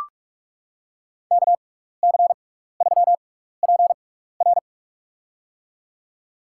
Morse code, heard as a keyed tone, about 45 wpm, spaced as for 15 wpm, 700 Hz. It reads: KC3PR
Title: USA Call Signs - 5 Characters - 6x Character Spacing - 45wpm